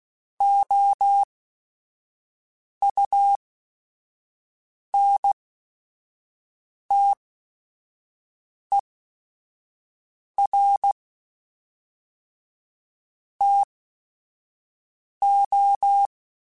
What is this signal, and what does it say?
Morse code, heard as a keyed tone, about 16 wpm, spaced as for 5 wpm, 800 Hz. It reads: OUNTER TO
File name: Technical Subjects / Technical Subjects-mp3s / XL @5